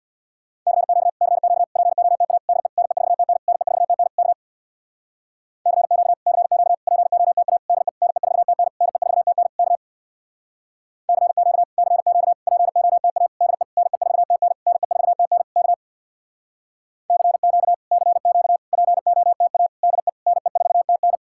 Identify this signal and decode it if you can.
Morse code, heard as a keyed tone, about 63 wpm, 700 Hz. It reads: CQ CQ POTA DE NE4TN NE4TN K CQ CQ POTA DE NE4TN NE4TN K CQ CQ POTA DE NE4TN NE4TN K CQ CQ POTA DE NE4TN